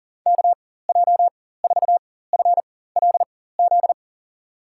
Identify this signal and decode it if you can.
Morse code, heard as a keyed tone, about 40 wpm, spaced as for 19 wpm, 700 Hz. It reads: KJ4FLZ
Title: USA Call Signs - 4x Character Spacing - 40wpm